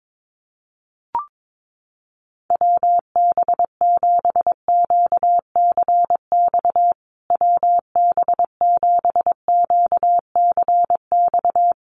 Morse code, heard as a keyed tone, about 22 wpm, 700 Hz. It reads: WB7QCX WB7QCX
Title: USA Call Signs Repeated 2x 22wpm